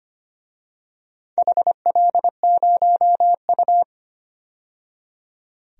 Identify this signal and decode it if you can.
Morse code, heard as a keyed tone, about 25 wpm, 700 Hz. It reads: HL0U
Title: DXCC-100 Call Signs - 4 Characters - 25wpm